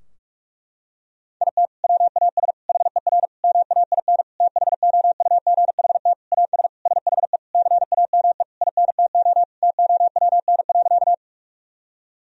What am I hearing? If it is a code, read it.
Morse code, heard as a keyed tone, about 41 wpm, 700 Hz. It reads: IT WAS HER MAIN THOUGHT AS SHE CAME INTO TOWN.
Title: Sentences from Top 1000 Words - Review Part 2 - 45wpm